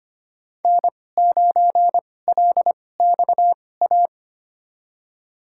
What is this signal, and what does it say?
Morse code, heard as a keyed tone, about 25 wpm, spaced as for 18 wpm, 700 Hz. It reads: N9LXA
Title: USA Call Signs - 5 Characters - 2x Character Spacing - 25wpm